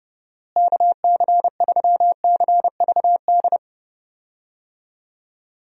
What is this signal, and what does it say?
Morse code, heard as a keyed tone, about 32 wpm, 700 Hz. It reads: KC3CVD
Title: USA Call Signs - 6 Characters - Repeated 3x - 30wpm